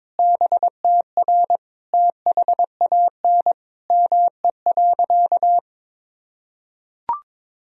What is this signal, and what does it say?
Morse code, heard as a keyed tone, about 22 wpm, 700 Hz. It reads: BTR THAN ME.
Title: Sentences from Top 400 Words and Abbreviations 22wpm